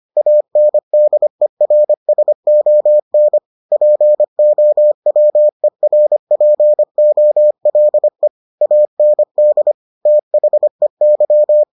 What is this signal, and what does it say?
Morse code, heard as a keyed tone, about 25 wpm, 600 Hz. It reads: ANDERSON POWERPOLE AND THEY